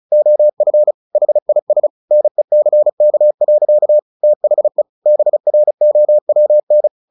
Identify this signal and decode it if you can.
Morse code, heard as a keyed tone, about 34 wpm, 600 Hz. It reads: OF HIS NECK. THE BROWN
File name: the old man and the sea / the old man and the sea 1 35wpm